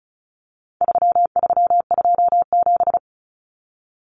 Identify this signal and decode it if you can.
Morse code, heard as a keyed tone, about 34 wpm, 700 Hz. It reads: 3327
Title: Common Cut Numbers - 4-Digit 35wpm